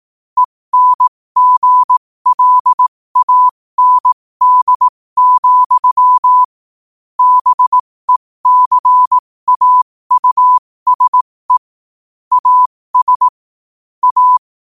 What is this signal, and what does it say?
Morse code, heard as a keyed tone, about 18 wpm, spaced as for 15 wpm, 1000 Hz. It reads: ENGLAND, BECAUSE AS A
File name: qotdfiles / 15wpm